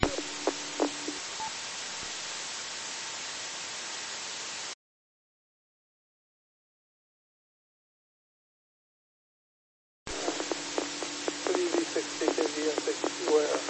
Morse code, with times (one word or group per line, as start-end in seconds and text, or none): none